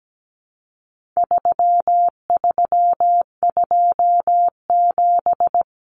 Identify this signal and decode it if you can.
Morse code, heard as a keyed tone, about 17 wpm, 700 Hz. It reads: 3327
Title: Common Cut Numbers - 4-Digit 17wpm